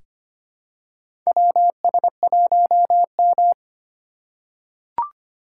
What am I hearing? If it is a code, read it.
Morse code, heard as a keyed tone, about 25 wpm, 700 Hz. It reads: WS1M
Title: USA Call Signs - 4 Characters - 4x Character Spacing - 25wpm